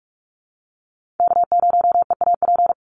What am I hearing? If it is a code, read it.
Morse code, heard as a keyed tone, about 44 wpm, 700 Hz. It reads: K0EAP
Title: NA SOTA Chaser Callsigns 45wpm